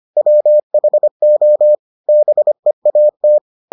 Morse code, heard as a keyed tone, about 25 wpm, 600 Hz. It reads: WHO BEAT